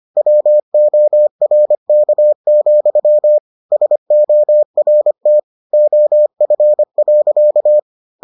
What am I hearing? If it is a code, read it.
Morse code, heard as a keyed tone, about 25 wpm, 600 Hz. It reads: WORK, SORT OF.